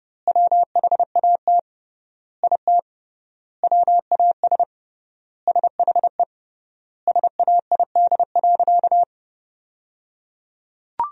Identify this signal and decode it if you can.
Morse code, heard as a keyed tone, about 29 wpm, 700 Hz. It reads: WHAT IT WAS SHE SAID.
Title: 3x Word-Spacing - Sentences from Top 400 Words - 30wpm